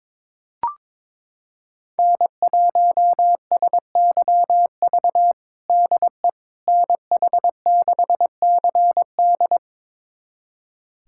Morse code, heard as a keyed tone, about 22 wpm, 700 Hz. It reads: N1SYV DE NH6CD
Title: USA Callsign DE Callsign - 5 Characters - 22wpm